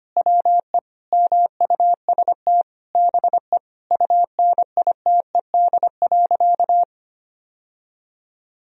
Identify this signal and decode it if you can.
Morse code, heard as a keyed tone, about 25 wpm, 700 Hz. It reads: WE MUST BE UNITED.
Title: Sentences from Top 1000 Words - Review Part 4 - 25wpm